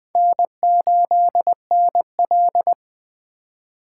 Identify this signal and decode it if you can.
Morse code, heard as a keyed tone, about 19 wpm, 700 Hz. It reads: N8NL